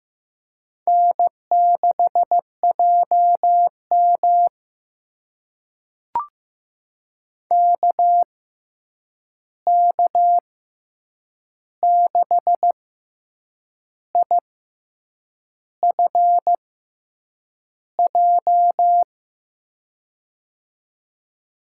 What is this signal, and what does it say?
Morse code, heard as a keyed tone, about 15 wpm, spaced as for 7 wpm, 700 Hz. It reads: N6JM KK6IFJ
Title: USA Call Signs - 6x Character Spacing - 15wpm